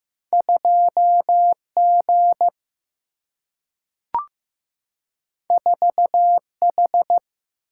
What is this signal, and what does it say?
Morse code, heard as a keyed tone, about 15 wpm, 700 Hz. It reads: 2G 4H
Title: Single-Letter Single-Number 15wpm